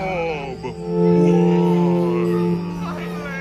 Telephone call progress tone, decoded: dial tone